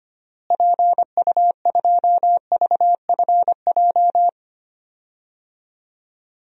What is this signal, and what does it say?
Morse code, heard as a keyed tone, about 25 wpm, 700 Hz. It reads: PU2VFJ
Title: DXCC-100 Call Signs - 6 Characters - Repeated 2x - 25wpm